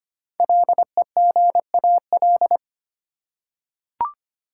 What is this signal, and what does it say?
Morse code, heard as a keyed tone, about 25 wpm, 700 Hz. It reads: LEGAL
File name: 5 Letter Words - First 3 Letters Given - 25wpm